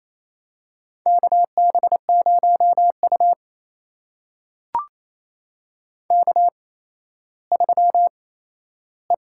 Morse code, heard as a keyed tone, about 28 wpm, spaced as for 14 wpm, 700 Hz. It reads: KB0U K3E
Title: USA Call Signs - 8x Character Spacing - 28wpm